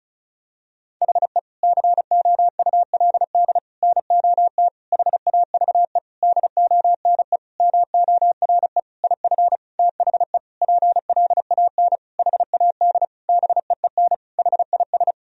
Morse code, heard as a keyed tone, about 34 wpm, 700 Hz. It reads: HE COULD NOT HAVE DONE MORE IF THE PLAN HAD BEEN HIS
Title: Sentences from Top 1000 Words - Review Part 2 - 35wpm